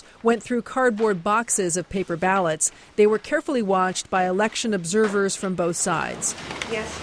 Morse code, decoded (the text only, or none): none